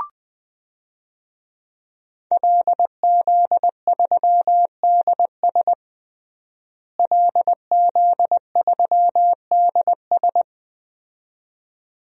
Morse code, heard as a keyed tone, about 20 wpm, 700 Hz. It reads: LZ3DS LZ3DS
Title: DXCC-100 Call Signs - 5 Characters - Repeated 2x - 20wpm